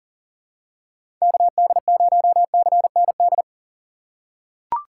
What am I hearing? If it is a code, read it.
Morse code, heard as a keyed tone, about 40 wpm, 700 Hz. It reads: KD0CND
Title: USA Call Signs - 6x Character Spacing - 40wpm